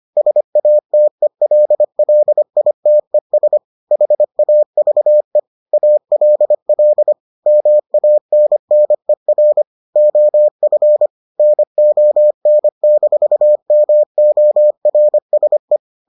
Morse code, heard as a keyed tone, about 25 wpm, 600 Hz. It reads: SATELLITES HAVE ALL MANNER OF NON-MORSE